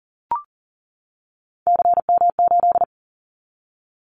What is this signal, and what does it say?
Morse code, heard as a keyed tone, about 40 wpm, 700 Hz. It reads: CM8
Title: Callsign Prefixes for Top 100 DXCC - 40wpm